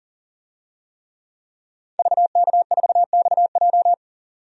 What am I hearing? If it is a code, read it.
Morse code, heard as a keyed tone, about 40 wpm, 700 Hz. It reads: VK4XJ
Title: DXCC-100 Call Signs - 5 Characters - 40wpm